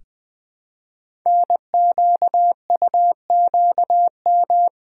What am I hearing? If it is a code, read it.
Morse code, heard as a keyed tone, about 20 wpm, 700 Hz. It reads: NQUQM
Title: Code Groups - 5 Letters - 20wpm